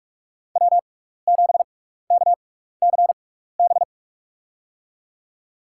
Morse code, heard as a keyed tone, about 44 wpm, spaced as for 15 wpm, 700 Hz. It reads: W7KCB
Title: USA Call Signs - 5 Characters - 6x Character Spacing - 45wpm